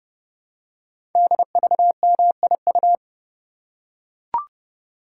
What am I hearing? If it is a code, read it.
Morse code, heard as a keyed tone, about 30 wpm, 700 Hz. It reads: DVMIU
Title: Code Groups - 5 Letters - 30wpm